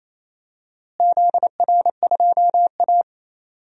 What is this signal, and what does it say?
Morse code, heard as a keyed tone, about 28 wpm, 700 Hz. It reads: ZR2A
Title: DXCC-100 Call Signs - 4 Characters - Repeated 3x - 28wpm